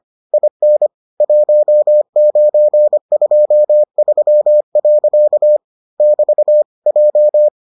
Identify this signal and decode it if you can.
Morse code, heard as a keyed tone, about 25 wpm, 600 Hz. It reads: IN 1923. <BT> J